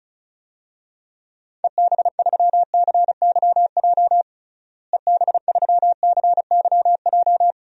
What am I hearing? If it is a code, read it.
Morse code, heard as a keyed tone, about 36 wpm, 700 Hz. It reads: EB3CYJ EB3CYJ
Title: DXCC-100 Call Signs - 6 Characters - Repeated 2x - 35wpm